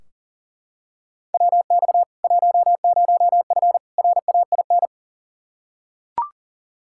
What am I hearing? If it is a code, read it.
Morse code, heard as a keyed tone, about 40 wpm, 700 Hz. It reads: WX 10F RAIN
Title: K1USN SST QSO Practice 40wpm